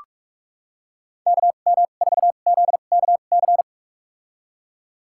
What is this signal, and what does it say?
Morse code, heard as a keyed tone, about 46 wpm, spaced as for 33 wpm, 700 Hz. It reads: KM4ZKC
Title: USA Call Signs - 6 Characters - 2x Character Spacing - 45wpm